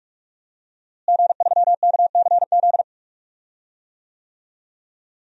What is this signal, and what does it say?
Morse code, heard as a keyed tone, about 44 wpm, 700 Hz. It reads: G3KCZ
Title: DXCC-100 Call Signs - 5 Characters - Repeated 2x - 45wpm